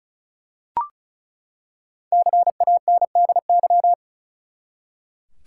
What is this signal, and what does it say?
Morse code, heard as a keyed tone, about 36 wpm, 700 Hz. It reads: CANDY
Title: Five Letter Words - 35wpm